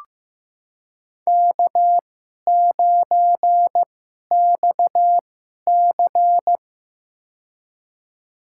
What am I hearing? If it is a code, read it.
Morse code, heard as a keyed tone, about 15 wpm, spaced as for 11 wpm, 700 Hz. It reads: K9XC
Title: USA Call Signs - 4 Characters - 2x Character Spacing - 15wpm